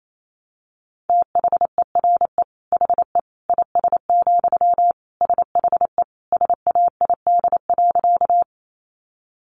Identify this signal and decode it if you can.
Morse code, heard as a keyed tone, about 28 wpm, 700 Hz. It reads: THERE HE IS, SHE SAID.
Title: Sentences from Top 100 Words 28wpm